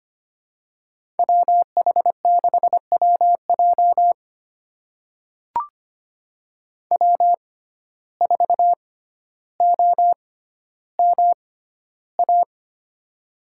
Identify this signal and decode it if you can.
Morse code, heard as a keyed tone, about 25 wpm, spaced as for 13 wpm, 700 Hz. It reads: WH6WJ W4OMA
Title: USA Call Signs - 5 Characters - 6x Character Spacing - 25wpm